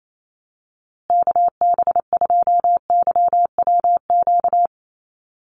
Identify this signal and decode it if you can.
Morse code, heard as a keyed tone, about 28 wpm, 700 Hz. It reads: KB2YWQ
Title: USA Call Signs Repeated 2x 28wpm